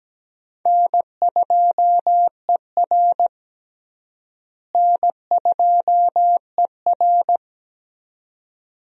Morse code, heard as a keyed tone, about 17 wpm, 700 Hz. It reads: N2ER N2ER
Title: USA Call Signs - 4 Characters - Repeated 3x - 17wpm